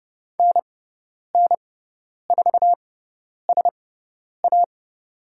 Morse code, heard as a keyed tone, about 30 wpm, spaced as for 10 wpm, 700 Hz. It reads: NN4SA
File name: Lesson 18g - Letter m - Call Signs 30wpm